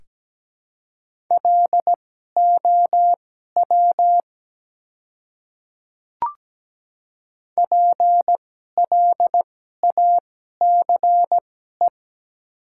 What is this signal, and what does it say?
Morse code, heard as a keyed tone, about 17 wpm, spaced as for 12 wpm, 700 Hz. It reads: LOW PLACE
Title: 2x Character Spacing - Top 200 Words - 17wpm